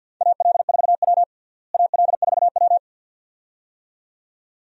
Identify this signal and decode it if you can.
Morse code, heard as a keyed tone, about 48 wpm, 700 Hz. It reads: AL4W AL4W
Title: USA Call Signs - 4 Characters - Repeated 2x - 50wpm